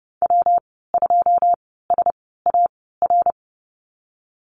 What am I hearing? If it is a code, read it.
Morse code, heard as a keyed tone, about 31 wpm, spaced as for 17 wpm, 700 Hz. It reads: W2SAR
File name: Lesson 19g - Letter w - Call Signs 30wpm